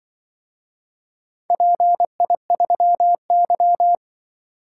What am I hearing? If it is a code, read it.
Morse code, heard as a keyed tone, about 24 wpm, 700 Hz. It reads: PI3Y
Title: Speed-Racing - Four Letter-Number Combinations - 20wpm